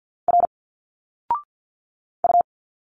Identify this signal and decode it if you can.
Morse code, heard as a keyed tone, about 50 wpm, 700 Hz. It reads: R U
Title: Single Letters 50wpm